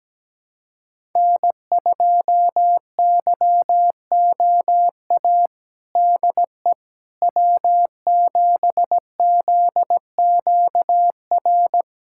Morse code, heard as a keyed tone, about 17 wpm, 700 Hz. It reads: N2YOA DE W7ZQR
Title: USA Callsign DE Callsign - 5 Characters - 17wpm